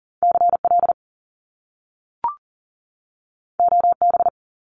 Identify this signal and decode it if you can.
Morse code, heard as a keyed tone, about 38 wpm, 700 Hz. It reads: CL OB
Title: Top Two Letter Combinations in Words 40wpm